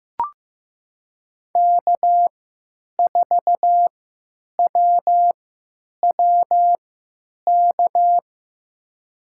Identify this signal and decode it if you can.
Morse code, heard as a keyed tone, about 15 wpm, spaced as for 9 wpm, 700 Hz. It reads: K4WWK
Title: USA Call Signs - 5 Characters - 3x Character Spacing - 15wpm